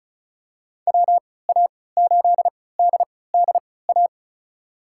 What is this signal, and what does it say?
Morse code, heard as a keyed tone, about 36 wpm, spaced as for 20 wpm, 700 Hz. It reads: WA8DDA
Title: USA Call Signs - 6 Characters - 3x Character Spacing - 35wpm